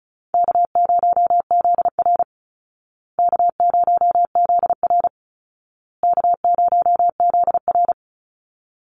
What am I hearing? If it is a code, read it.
Morse code, heard as a keyed tone, about 34 wpm, 700 Hz. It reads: K0ZR K0ZR K0ZR
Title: Easy Call Signs Repeated 3x - 35wpm